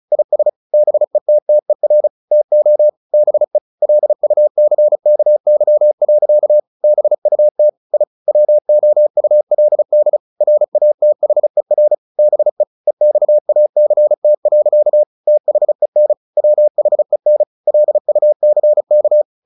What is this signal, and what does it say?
Morse code, heard as a keyed tone, about 34 wpm, 600 Hz. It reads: IS BETTER TO BE LUCKY. BUT I WOULD RATHER BE EXACT. THEN WHEN LUCK
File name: the old man and the sea / the old man and the sea 1 35wpm